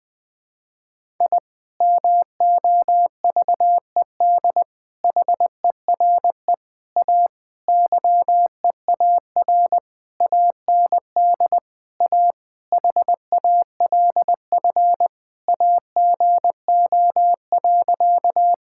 Morse code, heard as a keyed tone, about 20 wpm, 700 Hz. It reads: I MOVED HERE A YEAR AND A HALF AGO.
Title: Sentences with 6-Word Phrases - 20wpm